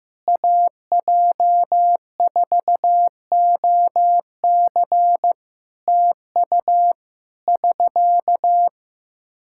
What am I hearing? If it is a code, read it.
Morse code, heard as a keyed tone, about 15 wpm, 700 Hz. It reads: AJ4OC TU <SK>
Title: Practice QSO - Exchange 4 - Ending - 15wpm